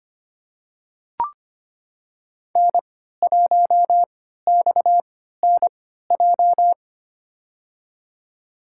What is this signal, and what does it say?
Morse code, heard as a keyed tone, about 25 wpm, spaced as for 14 wpm, 700 Hz. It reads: N1XNJ
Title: USA Call Signs - 5 Characters - 3x Character Spacing - 25wpm